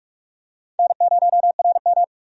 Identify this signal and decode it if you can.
Morse code, heard as a keyed tone, about 46 wpm, 700 Hz. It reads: N0RM
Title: USA Call Signs - 4 Characters - 45wpm